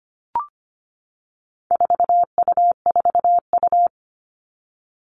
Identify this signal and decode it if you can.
Morse code, heard as a keyed tone, about 25 wpm, 700 Hz. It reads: 4U4U
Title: All Cut Numbers - 4-Digit 25wpm